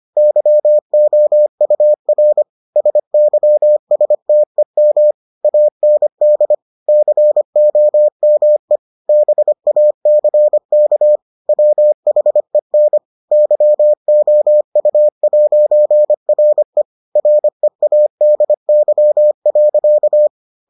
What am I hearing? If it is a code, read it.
Morse code, heard as a keyed tone, about 25 wpm, 600 Hz. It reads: YOUR SYSTEM AND COME BACK WHEN YOU'RE READY.